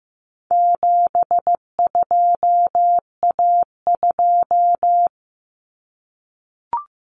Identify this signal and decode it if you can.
Morse code, heard as a keyed tone, about 15 wpm, 700 Hz. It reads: 72A2
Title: Common Cut Numbers - 4-Digit 15wpm